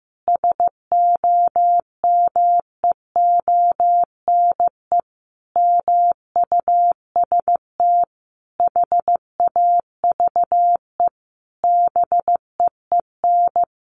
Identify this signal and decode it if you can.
Morse code, heard as a keyed tone, about 15 wpm, 700 Hz. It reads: SOMEONE MUST HAVE BEEN
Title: A Sound Of Thunder 15wpm